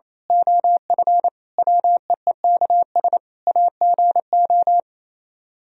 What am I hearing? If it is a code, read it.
Morse code, heard as a keyed tone, about 26 wpm, 700 Hz. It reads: OF WEEKS AGO
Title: Speed-Racing - Sets of 5 Words 28wpm